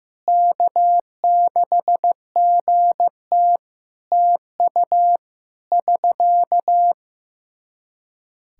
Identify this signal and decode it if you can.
Morse code, heard as a keyed tone, about 15 wpm, 700 Hz. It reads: K6GT TU <SK>
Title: Practice QSO - Exchange 4 - Ending - 15wpm